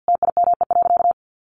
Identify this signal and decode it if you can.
Morse code, heard as a keyed tone, about 50 wpm, 700 Hz. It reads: TIME.
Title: Sentences from Top 200 Words 50wpm